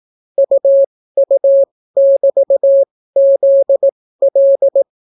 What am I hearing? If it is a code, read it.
Morse code, heard as a keyed tone, about 18 wpm, spaced as for 15 wpm, 550 Hz. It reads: UU<BT>ZL